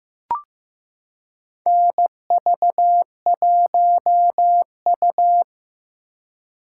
Morse code, heard as a keyed tone, about 16 wpm, 700 Hz. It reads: NV1U
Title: K1USN SST QSO Practice - 15wpm